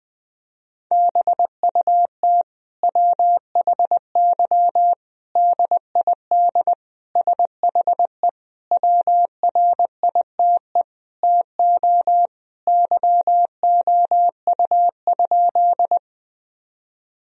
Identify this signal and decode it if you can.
Morse code, heard as a keyed tone, about 20 wpm, 700 Hz. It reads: BUT WHY DID SHE WRITE TO YOU?
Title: Sentences from Top 1000 Words - Review Part 1 - 20wpm